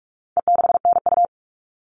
Morse code, heard as a keyed tone, about 44 wpm, 700 Hz. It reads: E6NU
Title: All Cut Numbers - 4-Digit 45wpm